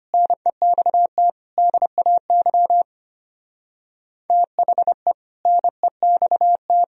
Morse code, heard as a keyed tone, about 30 wpm, 700 Hz. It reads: NEXT DAY THE NEXT
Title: Speed-Racing - Sets of 3 Words 25wpm